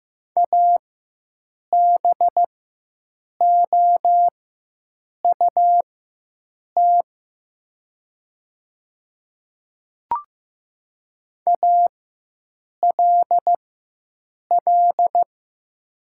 Morse code, heard as a keyed tone, about 15 wpm, spaced as for 7 wpm, 700 Hz. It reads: ABOUT ALL
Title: 4x Character Spacing - Top 100 Words - 15wpm